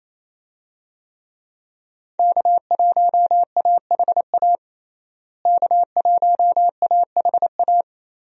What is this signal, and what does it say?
Morse code, heard as a keyed tone, about 28 wpm, 700 Hz. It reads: K1AHA K1AHA
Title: USA Call Signs - 5 Characters - Repeated 3x - 28wpm